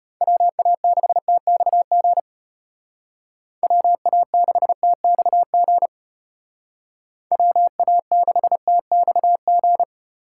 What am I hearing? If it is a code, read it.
Morse code, heard as a keyed tone, about 33 wpm, 700 Hz. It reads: WA6TXG WA6TXG WA6TXG